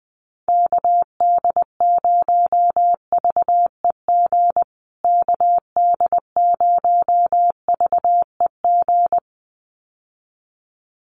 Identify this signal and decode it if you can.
Morse code, heard as a keyed tone, about 20 wpm, 700 Hz. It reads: KD0VEG KD0VEG
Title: USA Call Signs Repeated 2x 20wpm